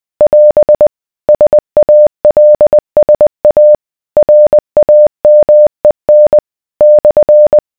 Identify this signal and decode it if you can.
Morse code, heard as a keyed tone, about 20 wpm, 600 Hz. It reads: <AS> SALSA RAMEN /